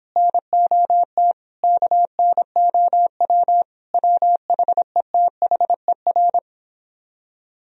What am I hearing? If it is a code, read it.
Morse code, heard as a keyed tone, about 25 wpm, 700 Hz. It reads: NOT KNOW WHETHER